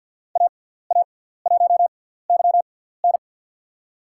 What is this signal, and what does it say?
Morse code, heard as a keyed tone, about 51 wpm, spaced as for 17 wpm, 700 Hz. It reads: AA1YN
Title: USA Call Signs - 6x Character Spacing - 50wpm